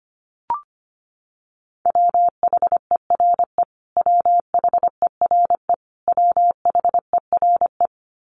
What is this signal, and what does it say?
Morse code, heard as a keyed tone, about 25 wpm, 700 Hz. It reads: WHERE WHERE WHERE
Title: Top 500 Words Repeated 25wpm